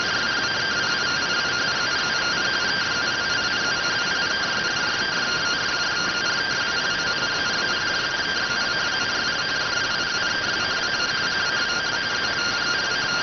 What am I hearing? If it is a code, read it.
Two-tone teleprinter signal, about 170 Hz shift.